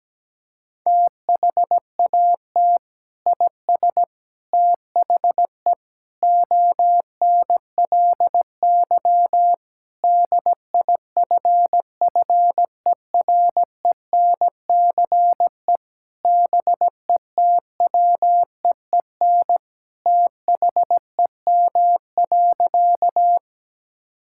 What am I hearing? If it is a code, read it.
Morse code, heard as a keyed tone, about 17 wpm, 700 Hz. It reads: THAT IS THE ONLY DIFFERENCE BETWEEN THEM.
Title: Sentences from Top 1000 Words - Review Part 5 - 17wpm